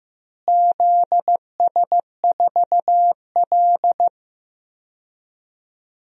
Morse code, heard as a keyed tone, about 15 wpm, 700 Hz. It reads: ZS4L
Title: DXCC-100 Call Signs - 4 Characters - Repeated 2x - 15wpm